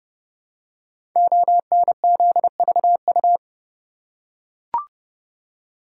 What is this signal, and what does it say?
Morse code, heard as a keyed tone, about 32 wpm, 700 Hz. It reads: ONZVU
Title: Code Groups - 5 Letter:Numbers - 30wpm